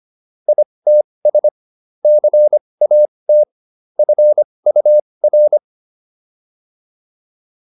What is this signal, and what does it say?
Morse code, heard as a keyed tone, about 25 wpm, spaced as for 20 wpm, 600 Hz. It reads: ITS CAT FUR